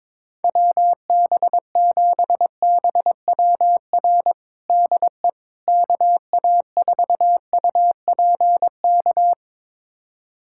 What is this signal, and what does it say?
Morse code, heard as a keyed tone, about 22 wpm, 700 Hz. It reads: WB7BWR DE KA4UPK
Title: USA Callsign DE Callsign 22wpm